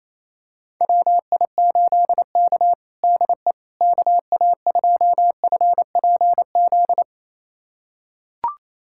Morse code, heard as a keyed tone, about 28 wpm, 700 Hz. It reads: WI8K DE KA2FPZ
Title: USA Callsign DE Callsign 28wpm